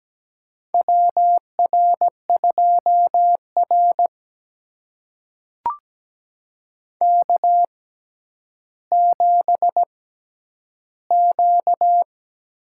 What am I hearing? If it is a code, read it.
Morse code, heard as a keyed tone, about 17 wpm, spaced as for 10 wpm, 700 Hz. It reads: WR2R K7Q
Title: USA Call Signs - 4 Characters - 6x Character Spacing - 17wpm